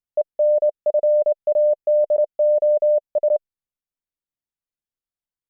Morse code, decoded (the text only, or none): ENFADOS